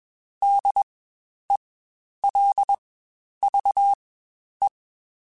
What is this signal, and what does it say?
Morse code, heard as a keyed tone, about 21 wpm, spaced as for 10 wpm, 800 Hz. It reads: DELVE
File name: Spaced Alphas / Spaced Alphas mp3s / Ds sp @20